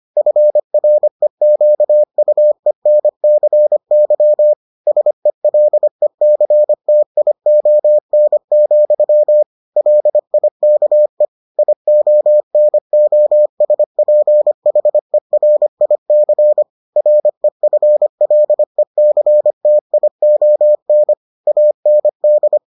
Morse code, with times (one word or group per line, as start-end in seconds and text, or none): 0.2-4.5 FREQUENCY
4.9-9.4 SELECTION,
9.8-11.3 LIKE
11.6-16.6 IONOSPHERIC
17.0-21.1 REFLECTION
21.5-22.6 AND